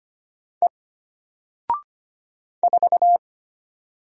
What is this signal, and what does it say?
Morse code, heard as a keyed tone, about 25 wpm, 700 Hz. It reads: E 4
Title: Single Letter-Number 25wpm